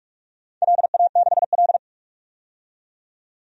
Morse code, heard as a keyed tone, about 44 wpm, 700 Hz. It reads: LA6L
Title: DXCC-100 Call Signs - 4 Characters - 45wpm